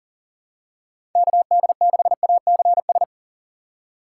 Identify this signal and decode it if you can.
Morse code, heard as a keyed tone, about 40 wpm, 700 Hz. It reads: KD6ACS
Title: USA Call Signs - 6 Characters - 6x Character Spacing - 40wpm